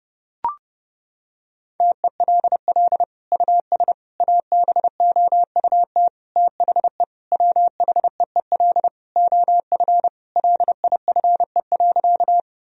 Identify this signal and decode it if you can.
Morse code, heard as a keyed tone, about 30 wpm, 700 Hz. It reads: TELL US ABOUT THE WHEEL OF LIFE.
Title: Sentences from Top 1000 Words - Review Part 3 - 30wpm